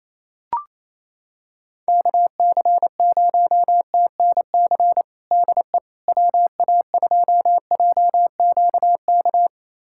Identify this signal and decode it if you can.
Morse code, heard as a keyed tone, about 27 wpm, 700 Hz. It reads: KC0TNC DE WA2JQK
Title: USA Callsign DE Callsign - 6 Characters - 28wpm